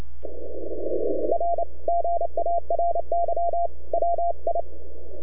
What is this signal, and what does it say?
Morse code, heard as a keyed tone, about 28 wpm, 680 Hz. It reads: R GARY WI
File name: NAQPCW-8-2009 / W1UJ NAQP CW 09 1930Z